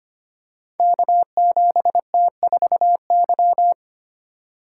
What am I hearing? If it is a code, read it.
Morse code, heard as a keyed tone, about 25 wpm, 700 Hz. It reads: K7T4Y
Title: Code Groups - 5 Letter:Numbers - 25wpm